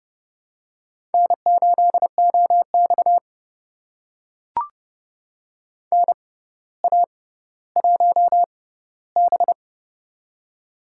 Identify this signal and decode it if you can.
Morse code, heard as a keyed tone, about 30 wpm, spaced as for 15 wpm, 700 Hz. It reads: N8OX NA1B
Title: USA Call Signs - 4 Characters - 6x Character Spacing - 30wpm